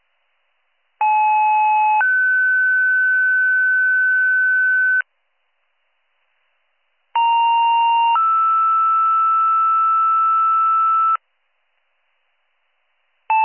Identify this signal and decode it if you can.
Two-tone page, heard as a one-second tone, then a three-second tone, about 860 then 1500 Hz, about 930 then 1320 Hz. It is heard from 1.0 to 11.2 s.